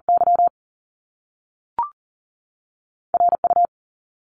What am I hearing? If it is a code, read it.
Morse code, heard as a keyed tone, about 42 wpm, 700 Hz. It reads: Y RU
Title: Top Two Letter Combinations in Words 40wpm